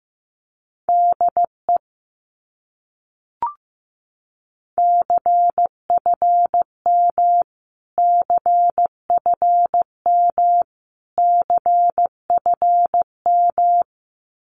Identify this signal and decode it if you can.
Morse code, heard as a keyed tone, about 15 wpm, 700 Hz. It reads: DE CFM CFM CFM
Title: Abbreviations Repeated 3x - 15wpm